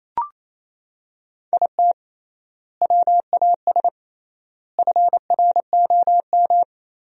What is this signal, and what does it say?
Morse code, heard as a keyed tone, about 27 wpm, 700 Hz. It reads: IT WAS FROM
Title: 3x Word-Spacing - Sentences from Top 100 Words - 28wpm